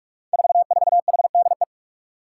Morse code, heard as a keyed tone, about 44 wpm, 700 Hz. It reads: 44HDE
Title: Five Challenging Letter-Number Combinations 45wpm